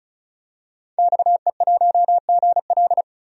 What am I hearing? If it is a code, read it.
Morse code, heard as a keyed tone, about 34 wpm, 700 Hz. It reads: XE1GL
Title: DXCC-100 Call Signs - 5 Characters - 35wpm